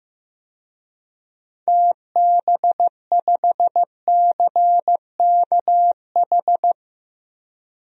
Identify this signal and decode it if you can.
Morse code, heard as a keyed tone, about 16 wpm, 700 Hz. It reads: TB5CKH
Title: DXCC-100 Call Signs - 6 Characters - 15wpm